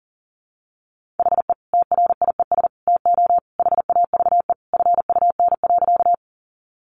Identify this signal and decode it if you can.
Morse code, heard as a keyed tone, about 40 wpm, 700 Hz. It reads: HE TRIES TO HAVE FUN.
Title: Easy Phrases 40wpm